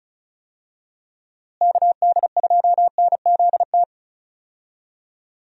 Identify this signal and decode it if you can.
Morse code, heard as a keyed tone, about 36 wpm, 700 Hz. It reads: KD2NZT
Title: USA Call Signs - 6 Characters - Repeated 2x - 35wpm